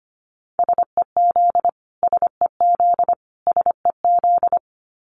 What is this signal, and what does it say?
Morse code, heard as a keyed tone, about 25 wpm, 700 Hz. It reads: SEZ SEZ SEZ
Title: Abbreviations Repeated 3x - 25wpm